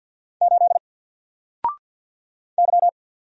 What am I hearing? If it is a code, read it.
Morse code, heard as a keyed tone, about 48 wpm, 700 Hz. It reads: 8 Y
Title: Single Character (Letter:Number:Punctuation:Prosign) Alt 50wpm